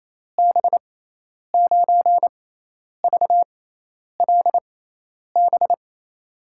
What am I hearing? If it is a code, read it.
Morse code, heard as a keyed tone, about 28 wpm, spaced as for 10 wpm, 700 Hz. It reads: B9VLB
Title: USA Call Signs - 6 Characters - 6x Character Spacing - 28wpm